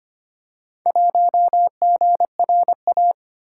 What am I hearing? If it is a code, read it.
Morse code, heard as a keyed tone, about 25 wpm, 700 Hz. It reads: 1GRA
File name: practice / 25wpm / Four Letter-Number Combinations - 25wpm